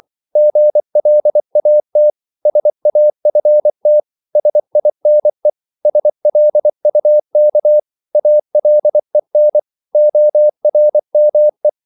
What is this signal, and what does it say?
Morse code, heard as a keyed tone, about 24 wpm, 600 Hz. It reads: GLAT SAFT SINE SLUK ALEN ORME
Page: Korte danske ord 24wpm